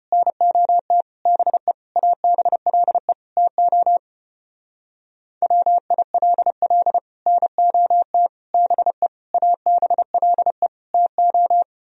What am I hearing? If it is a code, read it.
Morse code, heard as a keyed tone, about 33 wpm, 700 Hz. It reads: NOT BE ABLE TO WILL NOT BE ABLE TO